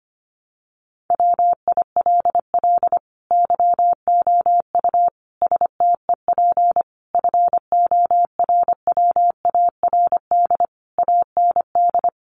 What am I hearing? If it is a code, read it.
Morse code, heard as a keyed tone, about 25 wpm, 700 Hz. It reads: WILL YOU STEP FORWARD AND
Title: Sentences from Top 1300 Words 25wpm